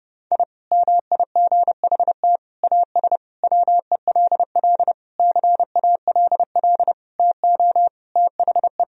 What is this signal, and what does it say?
Morse code, heard as a keyed tone, about 30 wpm, 700 Hz. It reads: I MIGHT AS WELL CALL TO THE
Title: Sentences from Top 1000 Words - Review Part 2 - 30wpm